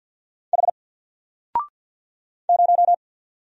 Morse code, heard as a keyed tone, about 50 wpm, 700 Hz. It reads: H 0
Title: Single Letter-Number 50wpm